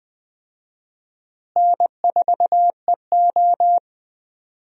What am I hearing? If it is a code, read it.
Morse code, heard as a keyed tone, about 20 wpm, 700 Hz. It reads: N4EO
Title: USA Call Signs - 4 Characters - Repeated 2x - 20wpm